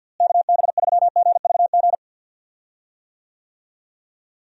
Morse code, heard as a keyed tone, about 50 wpm, 700 Hz. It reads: KB3GVG
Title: USA Call Signs - 6 Characters - Repeated 3x - 50wpm